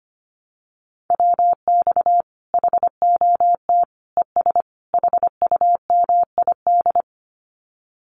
Morse code, heard as a keyed tone, about 25 wpm, 700 Hz. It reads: WX HOT ES HUMID
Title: Common QSO Phrases 25wpm